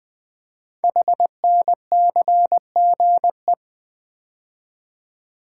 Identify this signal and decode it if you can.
Morse code, heard as a keyed tone, about 20 wpm, 700 Hz. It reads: HNCGE
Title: Code Groups - 5 Letters - 20wpm